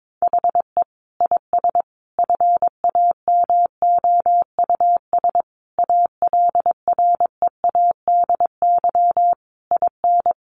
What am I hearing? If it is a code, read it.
Morse code, heard as a keyed tone, about 22 wpm, 700 Hz. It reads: HE IS FAMOUS ALREADY IN
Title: Sentences from Top 1500 Words 22wpm